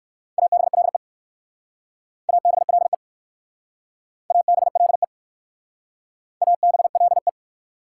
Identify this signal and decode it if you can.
Morse code, heard as a keyed tone, about 55 wpm, 690 Hz. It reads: ABLE ABLE ABLE ABLE
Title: Speed-Racing Prefixes and Suffixes 45wpm